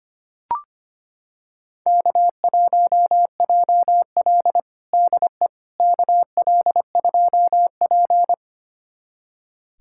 Morse code, heard as a keyed tone, about 25 wpm, 700 Hz. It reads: K1JL DE KL2P
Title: USA Callsign DE Callsign - 4 Characters - 25wpm